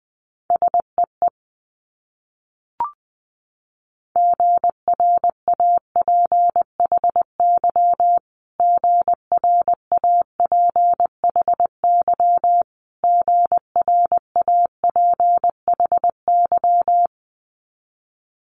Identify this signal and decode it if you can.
Morse code, heard as a keyed tone, about 20 wpm, 700 Hz. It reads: SEE GRAPHY GRAPHY GRAPHY
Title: Multi-letter Phonemes Repeated 3x 20wpm